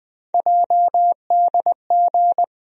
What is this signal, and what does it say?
Morse code, heard as a keyed tone, about 20 wpm, 700 Hz. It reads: JDG
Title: USA Call Sign Postfixes - 3 Letters 20wpm